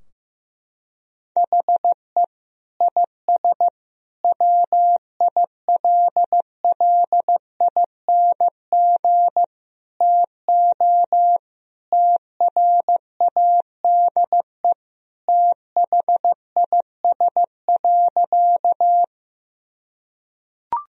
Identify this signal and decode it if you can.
Morse code, heard as a keyed tone, about 16 wpm, 700 Hz. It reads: HE IS WILLING TO TRADE THIS.
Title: Sentences from Top 1000 Words - Review Part 4 - 15wpm